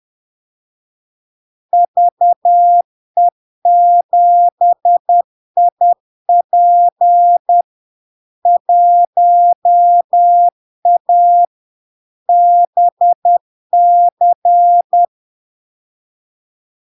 Morse code, heard as a keyed tone, about 10 wpm, 700 Hz. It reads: VE7IP 1A BC
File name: LARK-FD-2022-Log-in-CW-10wpm-700-Hz